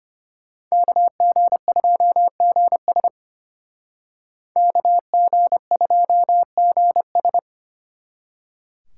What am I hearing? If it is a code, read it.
Morse code, heard as a keyed tone, about 32 wpm, 700 Hz. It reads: KG2GS KG2GS